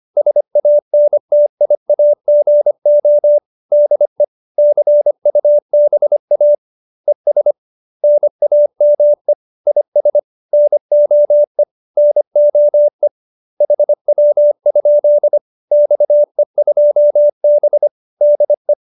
Morse code, heard as a keyed tone, about 25 wpm, 590 Hz. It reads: SANTIAGO DE CUBA ES NAME IS NOE NOE HW? XE2B DE
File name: QSO304 25